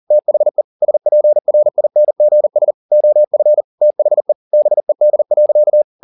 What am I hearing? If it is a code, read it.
Morse code, heard as a keyed tone, about 40 wpm, 600 Hz. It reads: THE SPRINGS OF THE BED.